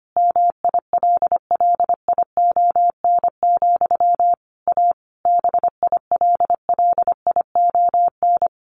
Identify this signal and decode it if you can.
Morse code, heard as a keyed tone, about 25 wpm, 700 Hz. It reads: MILLION, A BILLION
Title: A Sound Of Thunder 25wpm